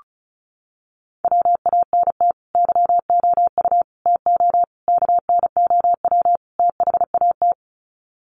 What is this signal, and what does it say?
Morse code, heard as a keyed tone, about 34 wpm, 700 Hz. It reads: WANT YOU TO KNOW THAT
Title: Sets of 5 Words Encore 35wpm